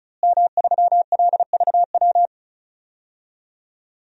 Morse code, heard as a keyed tone, about 36 wpm, 700 Hz. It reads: M3LVW
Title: DXCC-100 Call Signs - 5 Characters - Repeated 3x - 35wpm